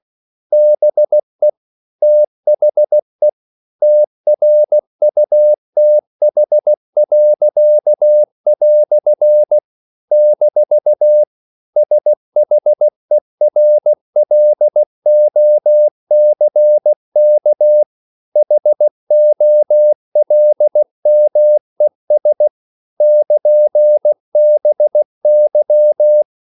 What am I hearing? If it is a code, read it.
Morse code, heard as a keyed tone, about 16 wpm, 600 Hz. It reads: BE THE TRUTH." - SHERLOCK HOLMES <KN>BY
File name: CW / Practice / EnglishQuotations0216